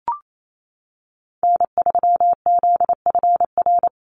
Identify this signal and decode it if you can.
Morse code, heard as a keyed tone, about 28 wpm, 700 Hz. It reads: N3ZFR
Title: USA Call Signs 28wpm